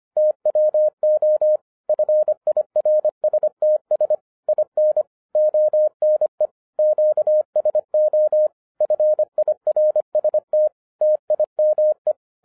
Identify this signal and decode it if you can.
Morse code, heard as a keyed tone, about 24 wpm, 610 Hz. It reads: TWO FIRSTS IN ONE QSO FIRST TIME
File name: FIRST FISTS CONTACT-25wpm